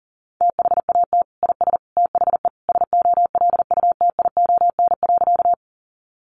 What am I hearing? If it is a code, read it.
Morse code, heard as a keyed tone, about 40 wpm, 700 Hz. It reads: THAT IS THE SOLUTION.
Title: Sentences from Top 1300 Words 40wpm